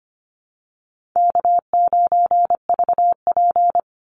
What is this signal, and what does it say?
Morse code, heard as a keyed tone, about 25 wpm, 700 Hz. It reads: K9VP
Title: Easy Call Signs Repeated 2x - 25wpm